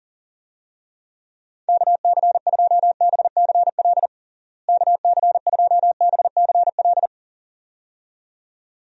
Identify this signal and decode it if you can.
Morse code, heard as a keyed tone, about 40 wpm, 700 Hz. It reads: KC2BCL KC2BCL
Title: USA Call Signs - 6 Characters - Repeated 2x - 40wpm